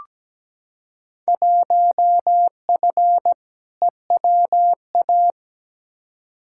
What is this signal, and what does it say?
Morse code, heard as a keyed tone, about 17 wpm, 700 Hz. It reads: 1F EWA
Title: Field Day Exchanges Encore - 17wpm